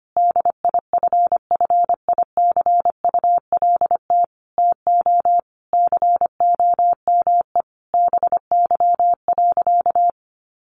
Morse code, heard as a keyed tone, about 25 wpm, 700 Hz. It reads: DIFFICULT TO COME BY.
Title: Sentences from Top 1500 Words 25wpm